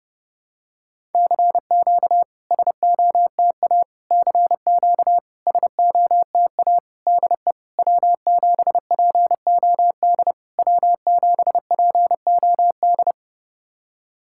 Morse code, heard as a keyed tone, about 28 wpm, 690 Hz. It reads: CQ SOTA CQ SOTA DE W7POD W7POD
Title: SOTA QSO Practice 30wpm